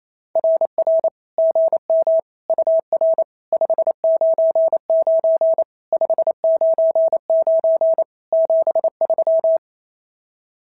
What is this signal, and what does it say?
Morse code, heard as a keyed tone, about 25 wpm, 650 Hz. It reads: RR GM UR 599 599 73
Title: SOTA QSO Practice 28wpm